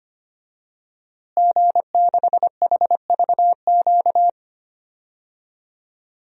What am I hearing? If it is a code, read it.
Morse code, heard as a keyed tone, about 25 wpm, 700 Hz. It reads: G6HVQ
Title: DXCC-100 Call Signs - 5 Characters - Repeated 3x - 25wpm